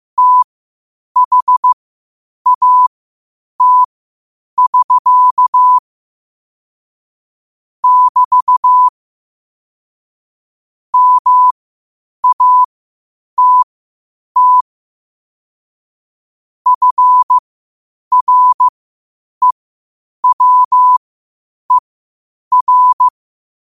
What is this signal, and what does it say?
Morse code, heard as a keyed tone, about 15 wpm, spaced as for 8 wpm, 1000 Hz. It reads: THAT<SK> <BT> MATT FREWER
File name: qotdfiles / 8wpm